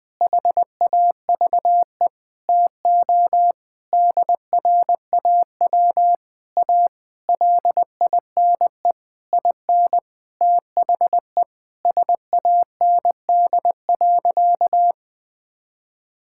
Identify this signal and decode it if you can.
Morse code, heard as a keyed tone, about 21 wpm, 700 Hz. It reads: HAVE TO DRAW A LINE IN THE SAND.
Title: Sentences with 7-Word Phrases - Spoken Beginning - 20wpm